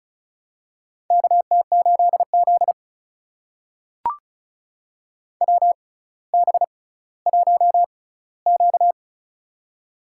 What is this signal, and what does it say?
Morse code, heard as a keyed tone, about 36 wpm, spaced as for 18 wpm, 700 Hz. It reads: KT8Z WB1Q
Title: USA Call Signs - 4 Characters - 6x Character Spacing - 35wpm